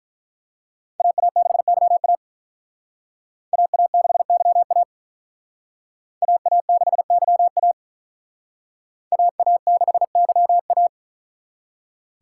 Morse code, heard as a keyed tone, about 43 wpm, 700 Hz. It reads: AA6YA AA6YA AA6YA AA6YA
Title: Speed Racing - USA 2x2 Call Signs 35wpm